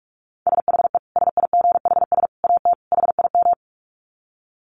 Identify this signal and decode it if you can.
Morse code, heard as a keyed tone, about 44 wpm, 700 Hz. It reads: SHE SIGHS AT HIM
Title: Dit Challenge 45wpm